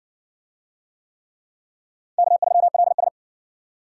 Morse code, heard as a keyed tone, about 58 wpm, 700 Hz. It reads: K3LS